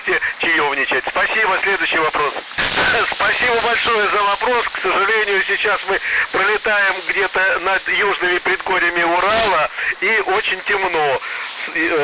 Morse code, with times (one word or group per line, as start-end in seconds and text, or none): none